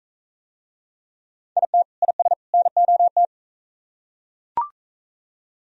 Morse code, heard as a keyed tone, about 41 wpm, 700 Hz. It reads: IT IS NOT